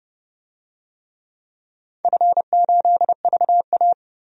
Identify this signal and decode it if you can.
Morse code, heard as a keyed tone, about 29 wpm, 700 Hz. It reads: F8VA
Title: DXCC-100 Call Signs - 4 Characters - 30wpm